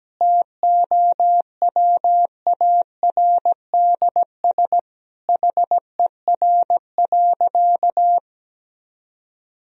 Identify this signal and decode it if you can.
Morse code, heard as a keyed tone, about 17 wpm, 700 Hz. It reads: TOWARDS HER.
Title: Sentences from Top 1000 Words - Review Part 2 - 17wpm